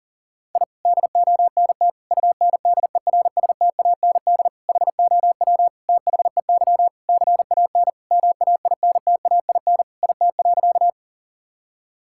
Morse code, heard as a keyed tone, about 40 wpm, 700 Hz. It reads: I DONT UNDERSTAND HOW THEY CAN MAINTAIN IT.
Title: Sentences from Top 1000 Words - Review Part 5 - 40wpm